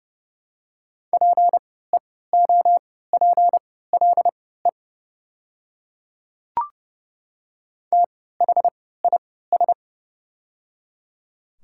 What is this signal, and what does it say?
Morse code, heard as a keyed tone, about 31 wpm, spaced as for 17 wpm, 700 Hz. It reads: PEOPLE THIS
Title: 3x Character Spacing - Top 100 Words - 30wpm